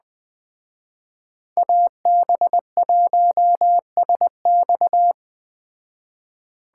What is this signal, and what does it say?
Morse code, heard as a keyed tone, about 21 wpm, 700 Hz. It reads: AB1SX
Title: Speed Racing - USA Call Signs 17wpm